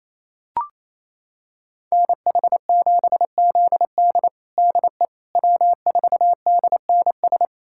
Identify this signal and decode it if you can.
Morse code, heard as a keyed tone, about 27 wpm, 700 Hz. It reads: NH7ZD DE W4DNS
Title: USA Callsign DE Callsign - 5 Characters - 28wpm